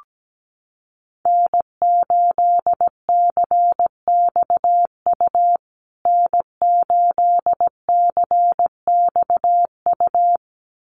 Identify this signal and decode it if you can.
Morse code, heard as a keyed tone, about 17 wpm, 700 Hz. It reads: N8CXU N8CXU
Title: USA Call Signs Repeated 2x 17wpm